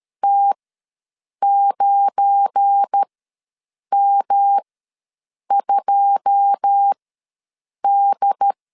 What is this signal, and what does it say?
Morse code, heard as a keyed tone, about 13 wpm, spaced as for 7 wpm, 800 Hz. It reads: T9M2D
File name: LEZIONI / 9 2 ws 12wpm 3406